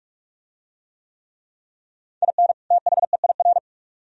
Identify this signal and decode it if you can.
Morse code, heard as a keyed tone, about 44 wpm, 700 Hz. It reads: IN THEIR